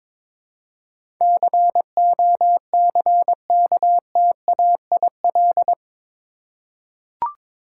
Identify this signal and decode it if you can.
Morse code, heard as a keyed tone, about 22 wpm, 700 Hz. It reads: COCKTAIL
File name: practice / 22wpm / Eight Letter Words - 22wpm